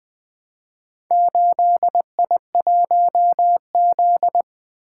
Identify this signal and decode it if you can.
Morse code, heard as a keyed tone, about 20 wpm, 700 Hz. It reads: 8I1Z
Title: Four Letter-Number Combinations - 20wpm